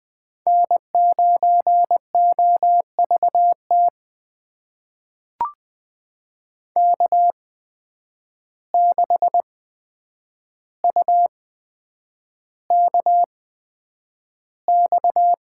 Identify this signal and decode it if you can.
Morse code, heard as a keyed tone, about 20 wpm, spaced as for 9 wpm, 700 Hz. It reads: N9OVT K6UKX
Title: USA Call Signs - 5 Characters - 8x Character Spacing - 20wpm